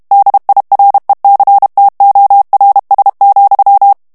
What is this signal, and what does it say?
Morse code, heard as a keyed tone, about 32 wpm, 800 Hz. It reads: DIRECTORS,
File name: morse mp3 / ANGLAIS texte clair 3 vitesse 30 mots minute